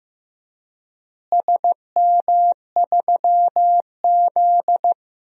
Morse code, heard as a keyed tone, about 15 wpm, 700 Hz. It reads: SM3Z
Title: DXCC-100 Call Signs - 4 Characters - Repeated 2x - 15wpm